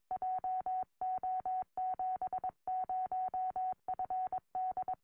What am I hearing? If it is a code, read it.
Morse code, heard as a keyed tone, about 21 wpm, 740 Hz. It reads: JO70FD
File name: au / la3eq 2m